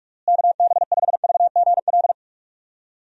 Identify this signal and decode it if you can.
Morse code, heard as a keyed tone, about 46 wpm, 700 Hz. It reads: KB5VGL
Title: USA Call Signs - 6 Characters - Repeated 3x - 45wpm